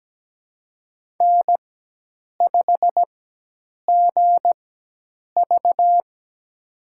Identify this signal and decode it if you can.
Morse code, heard as a keyed tone, about 17 wpm, spaced as for 8 wpm, 700 Hz. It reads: N5GV
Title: USA Call Signs - 4 Characters - 4x Character Spacing - 17wpm